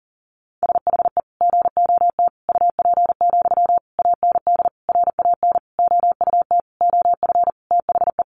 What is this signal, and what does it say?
Morse code, heard as a keyed tone, about 40 wpm, 700 Hz. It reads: SHE GOT UP, AND RAN OUT OF THE
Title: Sentences from Top 500 Words 40wpm